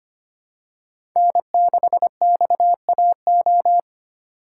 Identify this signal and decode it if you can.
Morse code, heard as a keyed tone, about 25 wpm, 700 Hz. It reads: N6XAO
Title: USA Call Signs - 5 Characters - 4x Character Spacing - 25wpm